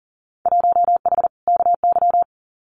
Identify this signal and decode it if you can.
Morse code, heard as a keyed tone, about 40 wpm, 700 Hz. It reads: 1H KY
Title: Winter Field Day Exchanges 40wpm